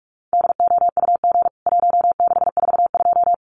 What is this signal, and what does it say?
Morse code, heard as a keyed tone, about 44 wpm, 700 Hz. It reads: DOUG 1642
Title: CWT Exchange 45wpm